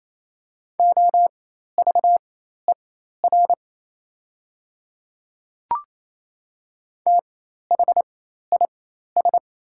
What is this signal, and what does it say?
Morse code, heard as a keyed tone, about 28 wpm, spaced as for 13 wpm, 700 Hz. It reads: OVER THIS
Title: 4x Character Spacing - Top 200 Words - 28wpm